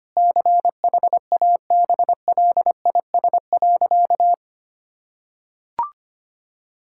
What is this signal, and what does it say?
Morse code, heard as a keyed tone, about 25 wpm, 700 Hz. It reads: CHABLIS.
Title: Pangram Challenge v2 - 25wpm